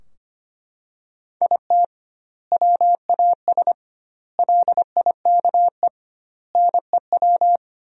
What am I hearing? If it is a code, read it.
Morse code, heard as a keyed tone, about 25 wpm, 700 Hz. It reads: IT WAS LIKE NEW
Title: 2x Word-Spacing - Sentences from Top 300 Words - 25wpm